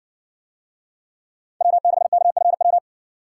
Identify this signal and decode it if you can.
Morse code, heard as a keyed tone, about 55 wpm, 700 Hz. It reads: W6KFW